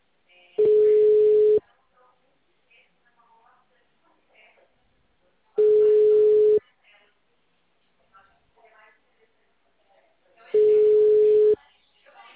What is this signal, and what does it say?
Ringback tone (Europe, 425 Hz), with speech.
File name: out- -2008- - - .15084